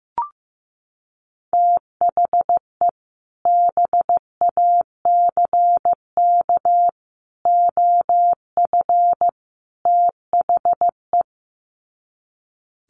Morse code, heard as a keyed tone, about 15 wpm, 700 Hz. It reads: THE BACK OF THE
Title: Sets of 4 Words 15wpm